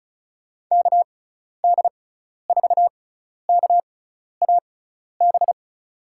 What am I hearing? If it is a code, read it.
Morse code, heard as a keyed tone, about 36 wpm, spaced as for 12 wpm, 700 Hz. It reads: KD4KAB
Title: USA Call Signs - 6 Characters - 6x Character Spacing - 35wpm